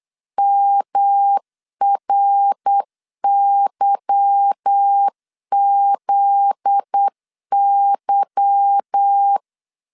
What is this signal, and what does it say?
Morse code, heard as a keyed tone, about 8 wpm, 800 Hz. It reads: MRYZY
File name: LEZIONI / 8 YJ45 / 8 Y 8wpm 3245